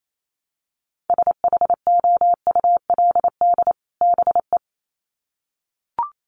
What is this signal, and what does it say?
Morse code, heard as a keyed tone, about 28 wpm, 700 Hz. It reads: SHOULD BE
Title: Sets of 2 Words 28wpm